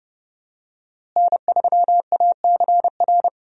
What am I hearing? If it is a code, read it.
Morse code, heard as a keyed tone, about 32 wpm, 700 Hz. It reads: N3ACR
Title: USA Call Signs - 5 Characters - Repeated 3x - 30wpm